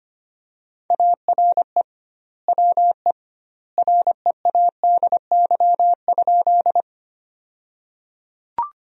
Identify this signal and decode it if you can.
Morse code, heard as a keyed tone, about 25 wpm, 700 Hz. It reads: ARE WE READY?
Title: 2x Word-Spacing - Sentences from Top 400 Words - 25wpm